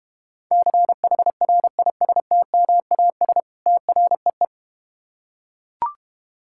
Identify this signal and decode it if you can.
Morse code, heard as a keyed tone, about 31 wpm, 700 Hz. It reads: CHRISTMAS TREE
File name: Easy Words - Speed-Racing with Hints - 25wpm